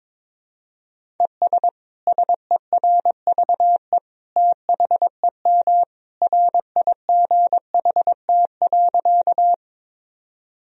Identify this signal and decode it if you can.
Morse code, heard as a keyed tone, about 21 wpm, 700 Hz. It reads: ES SERVE THEM RIGHT.
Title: Sentences from Top 400 Words and Abbreviations 22wpm